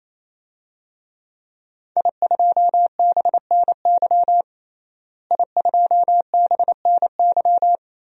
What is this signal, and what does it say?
Morse code, heard as a keyed tone, about 28 wpm, 700 Hz. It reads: I2BNY I2BNY
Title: DXCC-100 Call Signs - 5 Characters - Repeated 3x - 28wpm